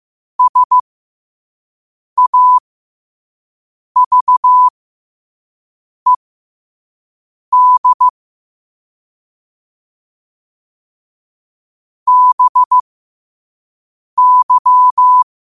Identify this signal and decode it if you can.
Morse code, heard as a keyed tone, about 15 wpm, spaced as for 5 wpm, 1000 Hz. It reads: SAVED BY
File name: qotdfiles / 5wpm